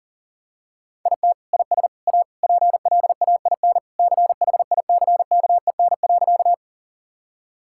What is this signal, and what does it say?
Morse code, heard as a keyed tone, about 40 wpm, 700 Hz. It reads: IT IS A PLAIN CHICKEN.
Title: Sentences from Top 1000 Words - Review Part 5 - 40wpm